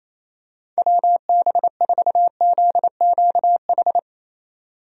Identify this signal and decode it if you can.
Morse code, heard as a keyed tone, about 28 wpm, 700 Hz. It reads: WB4ZQH
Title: USA Call Signs - 6 Characters - 28wpm